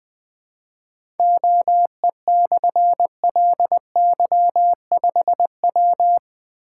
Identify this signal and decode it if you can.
Morse code, heard as a keyed tone, about 20 wpm, 700 Hz. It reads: OE/LY5W
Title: Difficult Call Signs 20wpm